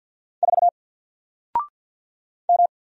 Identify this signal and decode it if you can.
Morse code, heard as a keyed tone, about 50 wpm, 700 Hz. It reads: 4 M
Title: Single Character (Letter:Number:Punctuation:Prosign) 50wpm